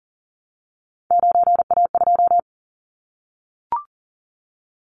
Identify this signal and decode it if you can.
Morse code, heard as a keyed tone, about 42 wpm, 700 Hz. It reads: 9A2
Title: Callsign Prefixes for Top 50 DXCC - 40wpm